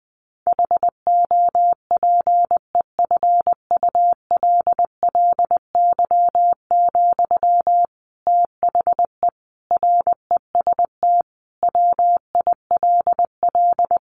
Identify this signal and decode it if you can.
Morse code, heard as a keyed tone, about 20 wpm, 700 Hz. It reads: HOPEFULLY, THE REST WILL
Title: Sentences from Top 1500 Words 20wpm